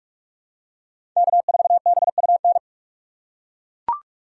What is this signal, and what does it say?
Morse code, heard as a keyed tone, about 41 wpm, 700 Hz. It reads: K4BUN
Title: USA Call Signs - 5 Characters - 4x Character Spacing - 45wpm